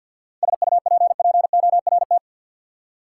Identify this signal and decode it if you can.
Morse code, heard as a keyed tone, about 52 wpm, 700 Hz. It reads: SUPPORT
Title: Practice QSO - Exchange 3 - More Info - 50wpm